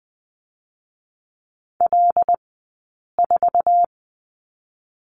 Morse code, heard as a keyed tone, about 20 wpm, 700 Hz. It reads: L 4
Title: 106 Lesson 18d - Letter m - Two Characters 20wpm